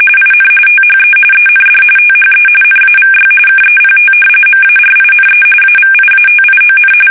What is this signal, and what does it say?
RTTY, about 45.45 baud. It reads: IN THE END THE NAVY KEPT MARINE AVIATION
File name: fsk850 / usnavy5